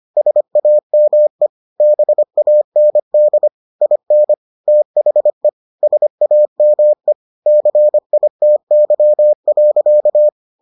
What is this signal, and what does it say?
Morse code, heard as a keyed tone, about 25 wpm, 600 Hz. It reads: SAME BAND IN THE SAME CITY.